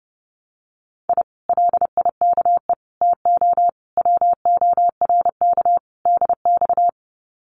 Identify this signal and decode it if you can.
Morse code, heard as a keyed tone, about 30 wpm, 700 Hz. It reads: I LIKE TO WORK DX
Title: Common QSO Phrases Encore 30wpm